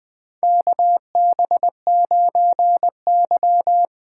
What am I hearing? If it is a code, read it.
Morse code, heard as a keyed tone, about 20 wpm, 700 Hz. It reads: KB9Y
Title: USA Call Signs - 4 Characters - 3x Character Spacing - 20wpm